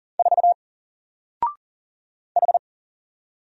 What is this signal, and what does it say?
Morse code, heard as a keyed tone, about 40 wpm, 700 Hz. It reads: 4 H
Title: Single Letter-Number 40wpm